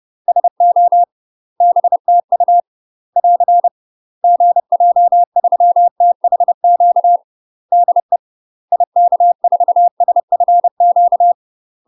Morse code, heard as a keyed tone, about 30 wpm, 700 Hz. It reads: SO BTU <AR> GJ3THQ DE IK4SFQ
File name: UK MASTER QSOs TRAINING / 30WPM / UK MASTER QSOs 0019